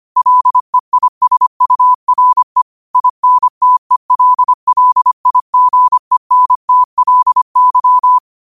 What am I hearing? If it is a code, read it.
Morse code, heard as a keyed tone, about 25 wpm, 1000 Hz. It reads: LEISURE INTELLIGENTLY
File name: qotdfiles / 25wpm